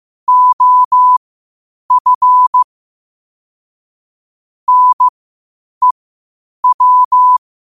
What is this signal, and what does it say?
Morse code, heard as a keyed tone, about 15 wpm, spaced as for 8 wpm, 1000 Hz. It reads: OF NEW
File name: qotdfiles / 8wpm